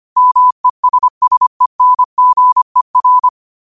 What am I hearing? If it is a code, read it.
Morse code, heard as a keyed tone, about 25 wpm, 1000 Hz. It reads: MESSENGER